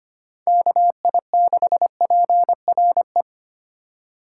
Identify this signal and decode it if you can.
Morse code, heard as a keyed tone, about 25 wpm, 700 Hz. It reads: KI6PRE
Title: USA Call Signs - 4x Character Spacing - 25wpm